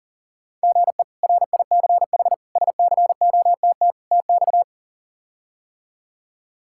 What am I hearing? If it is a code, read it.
Morse code, heard as a keyed tone, about 40 wpm, 700 Hz. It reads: GE RICH SCOTT TX
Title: K1USN SST QSO Practice - 40wpm